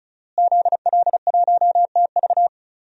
Morse code, heard as a keyed tone, about 36 wpm, 700 Hz. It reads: ZL1TV
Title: DXCC-100 Call Signs - 5 Characters - Repeated 3x - 35wpm